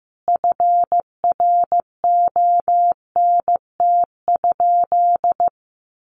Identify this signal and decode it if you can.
Morse code, heard as a keyed tone, about 15 wpm, 700 Hz. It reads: FRONT?
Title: Sentences from Top 1400 Words 15wpm